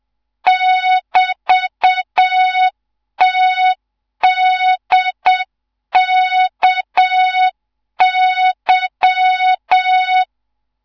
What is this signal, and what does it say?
Morse code, heard as a keyed tone, about 7 wpm, 3700 Hz, with noise. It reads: <BT>TDKY
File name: Morze-HA2MN / hang / 14 280 HA2MN